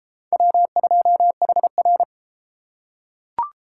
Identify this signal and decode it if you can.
Morse code, heard as a keyed tone, about 31 wpm, 700 Hz. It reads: W2HR